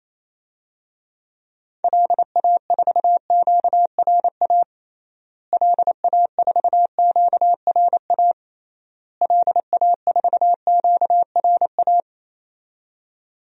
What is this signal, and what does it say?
Morse code, heard as a keyed tone, about 28 wpm, 700 Hz. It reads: LA4QRA LA4QRA LA4QRA
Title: DXCC-100 Call Signs - 6 Characters - Repeated 3x - 28wpm